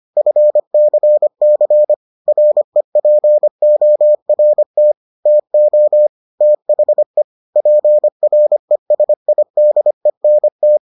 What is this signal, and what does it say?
Morse code, heard as a keyed tone, about 25 wpm, 600 Hz. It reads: FCC REPORT TO THE PRESIDENT